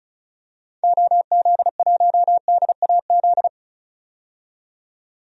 Morse code, heard as a keyed tone, about 36 wpm, 700 Hz. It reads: OZ1DAZ
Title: DXCC-100 Call Signs - 6 Characters - 35wpm